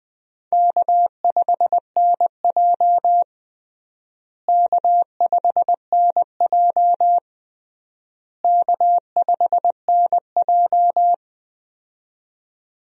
Morse code, heard as a keyed tone, about 21 wpm, 700 Hz. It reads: K5NJ K5NJ K5NJ
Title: USA Call Signs - 4 Characters - Repeated 3x - 20wpm